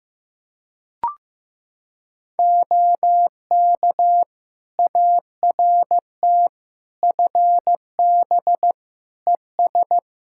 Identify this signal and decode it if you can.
Morse code, heard as a keyed tone, about 15 wpm, 700 Hz. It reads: OK ART FB ES
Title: Practice QSO - Exchange 2 - Info - 15wpm